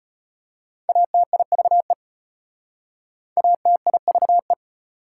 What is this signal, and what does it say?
Morse code, heard as a keyed tone, about 33 wpm, 700 Hz. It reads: ATIVE ATIVE